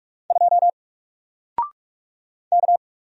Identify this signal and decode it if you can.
Morse code, heard as a keyed tone, about 46 wpm, 700 Hz. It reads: 2 K
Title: Single Letter-Number 45wpm